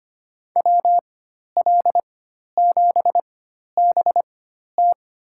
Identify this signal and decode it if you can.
Morse code, heard as a keyed tone, about 25 wpm, spaced as for 12 wpm, 700 Hz. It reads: WL7BT
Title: USA Call Signs - 4x Character Spacing - 25wpm